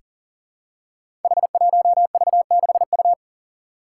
Morse code, heard as a keyed tone, about 41 wpm, 700 Hz. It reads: H1V6U
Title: Five Challenging Letter-Number Combinations 40wpm